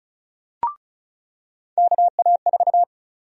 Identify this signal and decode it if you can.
Morse code, heard as a keyed tone, about 34 wpm, 700 Hz. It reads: KA4
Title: USA Call Sign Prefixes - 35wpm